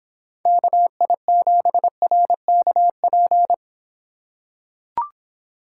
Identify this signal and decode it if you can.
Morse code, heard as a keyed tone, about 26 wpm, 700 Hz. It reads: KI7RKP